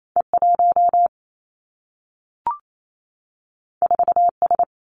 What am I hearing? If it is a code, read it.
Morse code, heard as a keyed tone, about 28 wpm, 700 Hz. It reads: E 1 4 S
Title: Two Challenging Letter-Number Combinations 28wpm